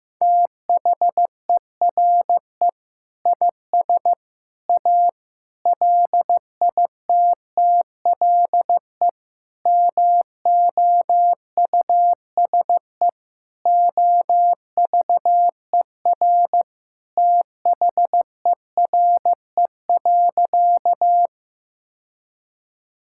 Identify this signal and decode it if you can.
Morse code, heard as a keyed tone, about 15 wpm, 700 Hz. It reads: THERE IS A LITTLE MOUSE OVER THERE.
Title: Sentences from Top 1000 Words - Review Part 5 - 15wpm